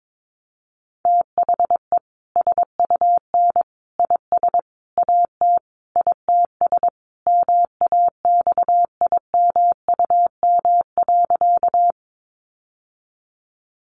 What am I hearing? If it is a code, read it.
Morse code, heard as a keyed tone, about 22 wpm, 700 Hz. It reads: THE SUN IS AT ITS MAXIMUM.
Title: Sentences from Top 1300 Words 22wpm